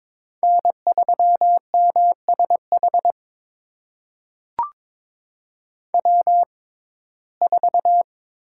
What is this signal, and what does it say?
Morse code, heard as a keyed tone, about 22 wpm, spaced as for 16 wpm, 700 Hz. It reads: N3MSH W4
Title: USA Call Signs - 5 Characters - 6x Character Spacing - 22wpm